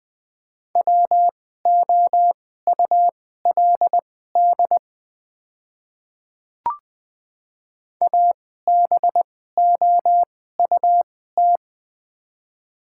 Morse code, heard as a keyed tone, about 20 wpm, spaced as for 14 wpm, 700 Hz. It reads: WOULD ABOUT
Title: 2x Character Spacing - Top 100 Words - 20wpm